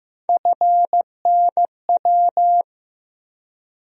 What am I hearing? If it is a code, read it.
Morse code, heard as a keyed tone, about 15 wpm, 700 Hz. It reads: FNW
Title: USA Call Sign Postfixes - 3 Letters 15wpm